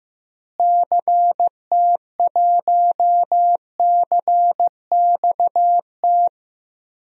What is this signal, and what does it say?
Morse code, heard as a keyed tone, about 16 wpm, 700 Hz. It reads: CT1CXT
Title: DXCC-100 Call Signs - 6 Characters - Repeated 3x - 15wpm